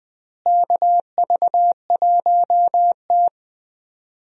Morse code, heard as a keyed tone, about 20 wpm, 700 Hz. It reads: KV1T
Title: USA Call Signs - 4 Characters - Repeated 3x - 20wpm